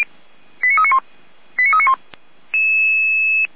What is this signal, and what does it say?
Five-tone selective call (ZVEI), heard as quick runs of five tones, tones of about 70 ms. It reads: 89391 89391
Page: /455/SDR/ZVEI/89033/